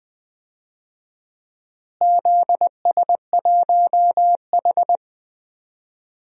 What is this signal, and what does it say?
Morse code, heard as a keyed tone, about 20 wpm, 700 Hz. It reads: ZS1H
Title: DXCC-100 Call Signs - 4 Characters - Repeated 3x - 20wpm